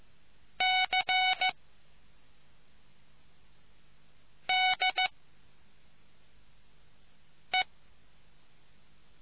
Morse code, heard as a keyed tone, about 15 wpm, 2250 Hz. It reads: C D E